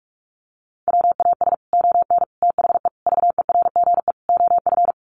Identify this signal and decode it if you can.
Morse code, heard as a keyed tone, about 44 wpm, 700 Hz. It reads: WAS ON THE VERGE OF
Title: Sets of 5 Words Encore 45wpm